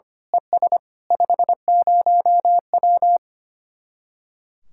Morse code, heard as a keyed tone, about 24 wpm, 700 Hz. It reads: ES 50W
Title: Speed-Racing Common QSO Phrases - 25wpm